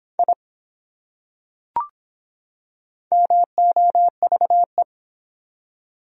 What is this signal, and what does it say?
Morse code, heard as a keyed tone, about 22 wpm, 700 Hz. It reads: I MOVE